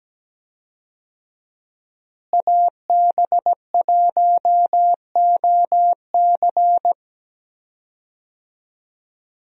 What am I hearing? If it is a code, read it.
Morse code, heard as a keyed tone, about 17 wpm, 700 Hz. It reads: AB1OC
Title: USA Call Signs - 5 Characters - Repeated 3x - 17wpm